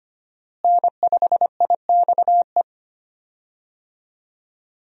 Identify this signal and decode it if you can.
Morse code, heard as a keyed tone, about 25 wpm, 700 Hz. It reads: N5IXE
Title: USA Call Signs - 5 Characters - Repeated 3x - 25wpm